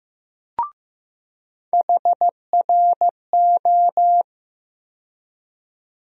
Morse code, heard as a keyed tone, about 15 wpm, 700 Hz. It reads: HRO
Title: USA Call Sign Postfixes - 3 Letters 15wpm